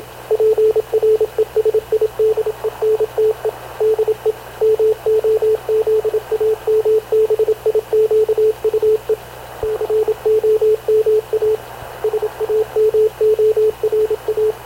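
Morse code, heard as a keyed tone, about 27 wpm, 430 Hz, with radio noise. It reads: PRESIDENTE DE MOZAMBIQUE COMA SAMORA
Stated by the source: Havana Radio / CLA sending Spanish press (PX) in Morse, recorded by Radio Officer Paul Dumesnil onboard CNAV QUEST/CZDO.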